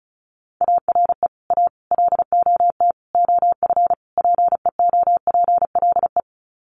Morse code, heard as a keyed tone, about 34 wpm, 700 Hz. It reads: ARE A LOT OF PEOPLE
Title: Sets of 5 Words 35wpm